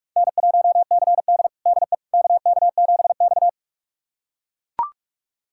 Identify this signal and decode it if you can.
Morse code, heard as a keyed tone, about 44 wpm, 700 Hz. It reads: N1CD DE KK7X
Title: USA Callsign DE Callsign - 4 Characters - 45wpm